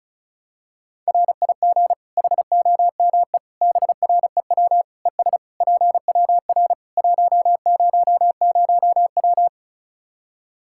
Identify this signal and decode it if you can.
Morse code, heard as a keyed tone, about 36 wpm, 700 Hz. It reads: RIG HOME BREW ES PWR 100W
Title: Practice QSO - Exchange 2 - Info - 35wpm